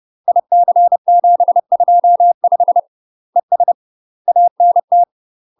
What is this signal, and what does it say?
Morse code, heard as a keyed tone, about 30 wpm, 700 Hz. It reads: IC725 ES ANT
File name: UK MASTER QSOs TRAINING / 30WPM / UK MASTER QSOs 0047